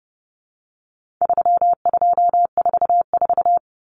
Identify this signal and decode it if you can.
Morse code, heard as a keyed tone, about 29 wpm, 700 Hz. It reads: 3244
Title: Common Cut Numbers - 4-Digit 30wpm